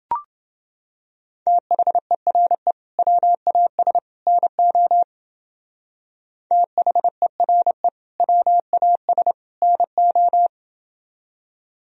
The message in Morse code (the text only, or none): THERE WAS NO THERE WAS NO